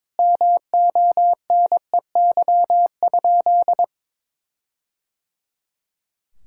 Morse code, heard as a keyed tone, about 22 wpm, 700 Hz. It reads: MONEY?
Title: Sentences from Top 1600 Words 22wpm